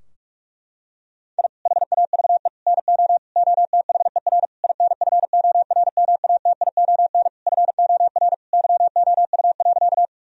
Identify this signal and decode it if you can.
Morse code, heard as a keyed tone, about 43 wpm, 700 Hz. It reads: I HAVE NO OTHER INFORMATION FOR YOU.
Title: Sentences from Top 1000 Words - Review Part 3 - 45wpm